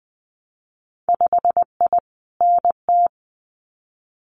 Morse code, heard as a keyed tone, about 20 wpm, 700 Hz. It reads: 5I NT
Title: Winter Field Day Exchanges Encore 20wpm